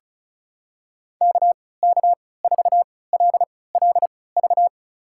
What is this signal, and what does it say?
Morse code, heard as a keyed tone, about 36 wpm, spaced as for 20 wpm, 700 Hz. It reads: KK4LLV
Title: USA Call Signs - 3x Character Spacing - 35wpm